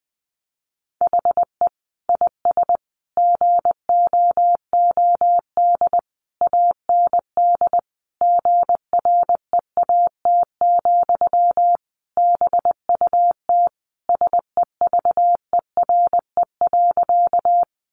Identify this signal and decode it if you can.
Morse code, heard as a keyed tone, about 19 wpm, 700 Hz. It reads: HE IS GOOD AND GREAT, BUT SEVERE.
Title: Sentences from Top 1400 Words 20wpm